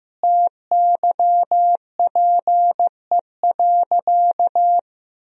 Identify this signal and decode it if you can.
Morse code, heard as a keyed tone, about 15 wpm, 700 Hz. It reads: TYPE.
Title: Sentences from Top 1000 Words - Review Part 3 - 15wpm